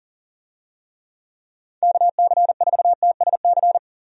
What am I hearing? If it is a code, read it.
Morse code, heard as a keyed tone, about 40 wpm, 700 Hz. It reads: KC4TSC
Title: USA Call Signs - 6 Characters - 40wpm